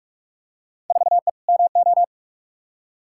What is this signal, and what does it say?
Morse code, heard as a keyed tone, about 46 wpm, 700 Hz. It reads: 4E MO
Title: Field Day Exchanges Encore - 45wpm